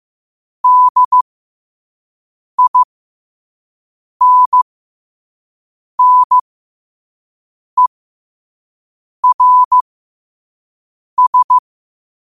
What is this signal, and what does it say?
Morse code, heard as a keyed tone, about 15 wpm, spaced as for 5 wpm, 1000 Hz. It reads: DINNERS
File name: qotdfiles / 5wpm